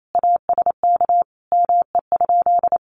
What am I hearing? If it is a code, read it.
Morse code, heard as a keyed tone, about 28 wpm, 700 Hz. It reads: ASK ME?
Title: Sentences from Top 1000 Words 28wpm